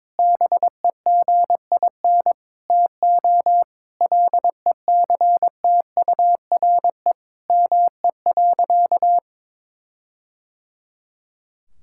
Morse code, heard as a keyed tone, about 22 wpm, 700 Hz. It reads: BEGIN TO LECTURE ME.
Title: Sentences from Top 1600 Words 22wpm